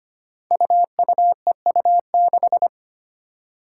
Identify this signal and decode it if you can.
Morse code, heard as a keyed tone, about 25 wpm, 700 Hz. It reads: UUEU6
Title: Five Challenging Letter-Number Combinations 25wpm